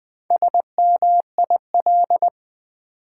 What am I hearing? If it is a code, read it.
Morse code, heard as a keyed tone, about 20 wpm, 700 Hz. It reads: SMIL
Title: Build A Word - 1 Letter At A Time - 3 to 5-Letter Words - 20wpm